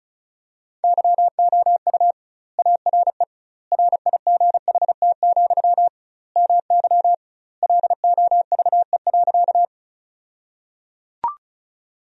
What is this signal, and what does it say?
Morse code, heard as a keyed tone, about 36 wpm, 700 Hz. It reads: YOU ARE RIGHT, MY LOVE.
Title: 2x Word-Spacing - Sentences from Top 400 Words - 35wpm